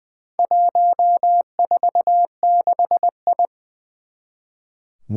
Morse code, heard as a keyed tone, about 20 wpm, 700 Hz. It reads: 146I
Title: Four Challenging Letter-Number Combinations 20wpm